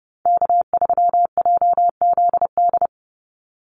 Morse code, heard as a keyed tone, about 30 wpm, 700 Hz. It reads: K3JZD
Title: NA SOTA Chaser Callsigns 30wpm